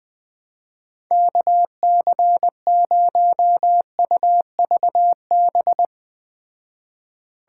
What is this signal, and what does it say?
Morse code, heard as a keyed tone, about 20 wpm, 700 Hz. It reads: KC0UVB